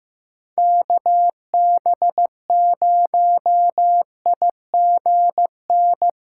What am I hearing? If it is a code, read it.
Morse code, heard as a keyed tone, about 15 wpm, 700 Hz. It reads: KB0IGN
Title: USA Call Signs - 6 Characters - 3x Character Spacing - 15wpm